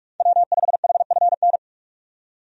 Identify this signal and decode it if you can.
Morse code, heard as a keyed tone, about 46 wpm, 700 Hz. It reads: W5HFN
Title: USA Call Signs - 5 Characters - 45wpm